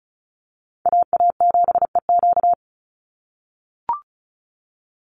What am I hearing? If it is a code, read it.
Morse code, heard as a keyed tone, about 34 wpm, 700 Hz. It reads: AA7EQ
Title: NA SOTA Chaser Callsigns 35wpm